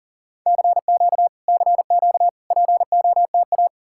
Morse code, heard as a keyed tone, about 51 wpm, 700 Hz. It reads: CQ CQ POTA